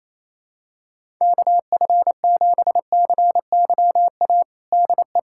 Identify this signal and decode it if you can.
Morse code, heard as a keyed tone, about 28 wpm, 700 Hz. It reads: KF7CYA DE
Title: USA Callsign DE Callsign - 6 Characters - 28wpm